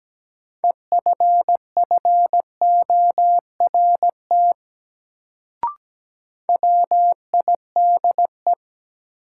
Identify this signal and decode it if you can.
Morse code, heard as a keyed tone, about 17 wpm, 700 Hz. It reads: EFFORT WIDE
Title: Sets of 4 Words - New Word Drill - 17wpm